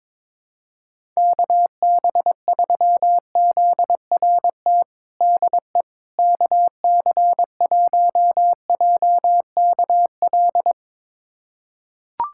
Morse code, heard as a keyed tone, about 22 wpm, 700 Hz. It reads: KB3ZRT DE KC1JKL
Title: USA Callsign DE Callsign 22wpm